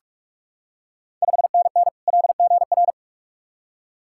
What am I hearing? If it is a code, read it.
Morse code, heard as a keyed tone, about 44 wpm, 690 Hz. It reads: 5NN LGR
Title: Ontario QSO Party Exchange 45wpm